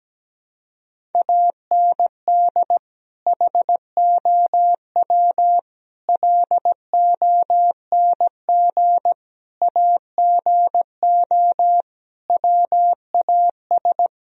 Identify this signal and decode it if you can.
Morse code, heard as a keyed tone, about 17 wpm, 700 Hz. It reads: AND HOW LONG AGO WAS
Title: Sentences from Top 1000 Words - Review Part 3 - 17wpm